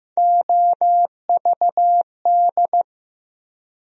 Morse code, heard as a keyed tone, about 15 wpm, 700 Hz. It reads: OVD
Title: USA Call Sign Postfixes - 3 Letters 15wpm